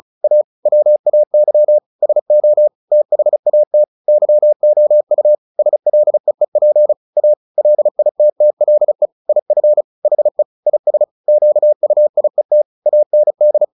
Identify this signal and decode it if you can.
Morse code, heard as a keyed tone, about 34 wpm, 600 Hz. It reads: A WAY SO THAT YOU SLEEP A LITTLE IF HE IS QUIET AND
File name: the old man and the sea 2 35wpm